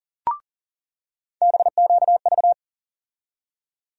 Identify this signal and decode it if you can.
Morse code, heard as a keyed tone, about 37 wpm, 700 Hz. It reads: BQV
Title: USA Call Sign Postfixes - 3 Letters 40wpm